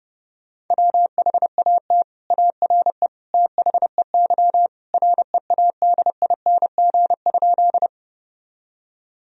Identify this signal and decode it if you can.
Morse code, heard as a keyed tone, about 30 wpm, 700 Hz. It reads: WHAT ARE THEY READING?
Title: Sentences from Top 1000 Words - Review Part 3 - 30wpm